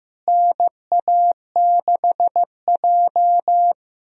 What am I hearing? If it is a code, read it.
Morse code, heard as a keyed tone, about 15 wpm, 700 Hz. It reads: NA6J
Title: USA Call Signs - 4 Characters - Repeated 3x - 15wpm